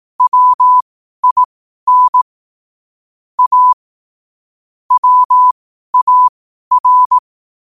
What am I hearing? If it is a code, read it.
Morse code, heard as a keyed tone, about 18 wpm, spaced as for 12 wpm, 1000 Hz. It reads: WIN A WAR
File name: qotdfiles / 12wpm